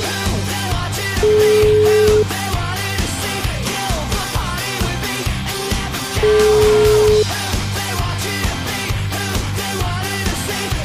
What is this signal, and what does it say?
Ringback tone (Europe, 425 Hz), with noise.